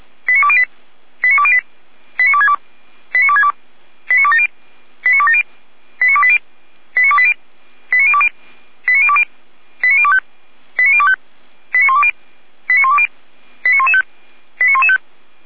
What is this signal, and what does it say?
Five-tone selective call (ZVEI), heard as quick runs of five tones, tones of about 80 ms. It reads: 89208 89208 89262 89262 89270 89270 89280 89280 89020 89020 89025 89025 89120 89120 89115 89115